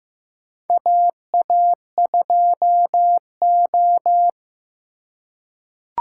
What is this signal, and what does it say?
Morse code, heard as a keyed tone, about 15 wpm, 700 Hz. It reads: AA2O
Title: USA Call Signs - 3x Character Spacing - 15wpm